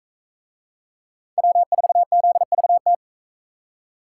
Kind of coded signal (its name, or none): Morse code